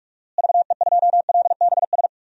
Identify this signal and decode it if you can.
Morse code, heard as a keyed tone, about 44 wpm, 700 Hz. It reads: VE2LBS
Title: DXCC-100 Call Signs - 6 Characters - Repeated 2x - 45wpm